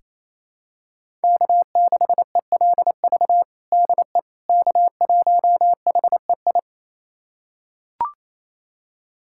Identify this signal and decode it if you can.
Morse code, heard as a keyed tone, about 27 wpm, 700 Hz. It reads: K6ELV DE K1HEI
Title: USA Callsign DE Callsign - 5 Characters - 28wpm